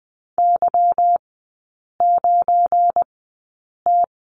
Morse code, heard as a keyed tone, about 20 wpm, spaced as for 8 wpm, 700 Hz. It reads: Y9T
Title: 172 Lesson 28e - Number 9 - Three Characters 20wpm